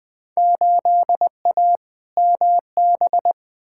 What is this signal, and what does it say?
Morse code, heard as a keyed tone, about 20 wpm, 700 Hz. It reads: 8A MB
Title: Field Day Exchanges Encore - 20wpm